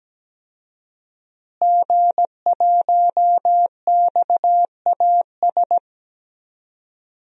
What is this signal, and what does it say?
Morse code, heard as a keyed tone, about 17 wpm, 700 Hz. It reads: G1XAS
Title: DXCC-100 Call Signs - 5 Characters - Repeated 3x - 17wpm